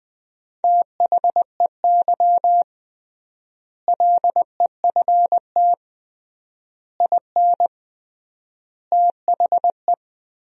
Morse code, heard as a keyed tone, about 20 wpm, 700 Hz. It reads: THEY LEFT IN THE
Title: 3x Word-Spacing - Sentences from Top 400 Words - 20wpm